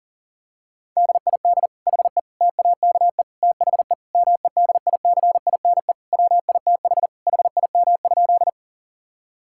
Morse code, heard as a keyed tone, about 37 wpm, 700 Hz. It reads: DID HE TAKE THE MEDICINE WITH HIM?
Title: Sentences from Top 1000 Words - Review Part 6 - 40wpm